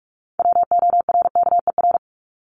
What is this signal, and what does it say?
Morse code, heard as a keyed tone, about 44 wpm, 700 Hz. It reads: WORKER
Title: Top 1751-2000 Words 45wpm